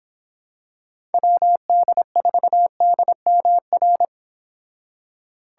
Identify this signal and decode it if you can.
Morse code, heard as a keyed tone, about 25 wpm, 700 Hz. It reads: WD4DMR